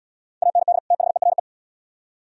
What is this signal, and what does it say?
Morse code, heard as a keyed tone, about 73 wpm, 700 Hz. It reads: AND THERE